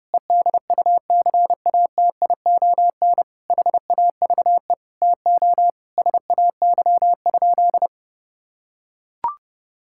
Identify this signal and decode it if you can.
Morse code, heard as a keyed tone, about 32 wpm, 700 Hz. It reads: EDUCATION HAVE TO SAY?
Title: Sentences from Top 1000 Words - Review Part 6 - 30wpm